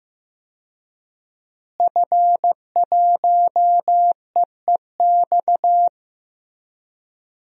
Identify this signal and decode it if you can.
Morse code, heard as a keyed tone, about 15 wpm, 700 Hz. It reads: F1EEX
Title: DXCC-100 Call Signs - 5 Characters - Repeated 2x - 15wpm